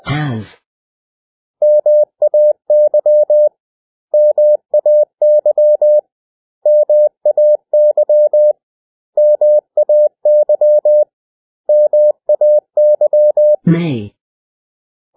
Morse code, heard as a keyed tone, about 20 wpm, 600 Hz. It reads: MAY MAY MAY MAY MAY
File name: cw / 100 Common English Words